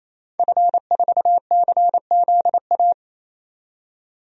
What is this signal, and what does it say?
Morse code, heard as a keyed tone, about 28 wpm, 700 Hz. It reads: F4CZA
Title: DXCC-100 Call Signs - 5 Characters - 28wpm